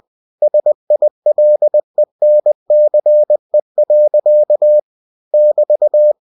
Morse code, heard as a keyed tone, about 20 wpm, 600 Hz. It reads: SILENCE. <BT>
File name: cw-books / winnie-the-pooh / winnie-20wpm-page-0027